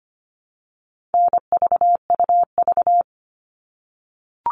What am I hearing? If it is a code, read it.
Morse code, heard as a keyed tone, about 25 wpm, 700 Hz. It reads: NVUV
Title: All Cut Numbers - 4-Digit 25wpm